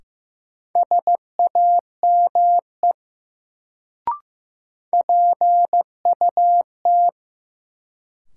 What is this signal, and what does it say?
Morse code, heard as a keyed tone, about 15 wpm, 700 Hz. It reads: SAME PUT
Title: Sentences from Top 200 Words - New Word Drill - 15wpm